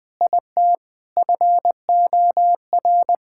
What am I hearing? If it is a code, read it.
Morse code, heard as a keyed tone, about 20 wpm, 700 Hz. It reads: IT FOR
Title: Sentences from Top 1000 Words - Review Part 6 - 20wpm